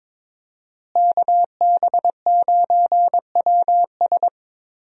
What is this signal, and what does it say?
Morse code, heard as a keyed tone, about 22 wpm, 700 Hz. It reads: KB9WS
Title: USA Call Signs - 5 Characters - Repeated 2x - 22wpm